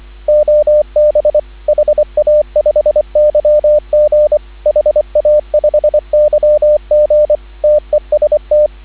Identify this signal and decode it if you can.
Morse code, heard as a keyed tone, about 24 wpm, 590 Hz, with radio noise. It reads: OB HA5YG HA5YG TEST
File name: CW / 2017 HA-OB HA5YG